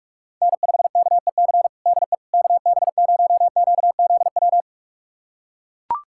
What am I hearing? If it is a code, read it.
Morse code, heard as a keyed tone, about 41 wpm, 700 Hz. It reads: N5KEC DE KB0QZW
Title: USA Callsign DE Callsign 45wpm